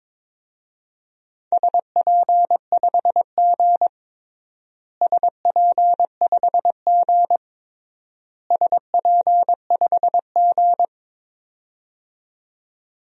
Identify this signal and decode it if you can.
Morse code, heard as a keyed tone, about 22 wpm, 700 Hz. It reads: SP5G SP5G SP5G
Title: DXCC-100 Call Signs - 4 Characters - Repeated 3x - 22wpm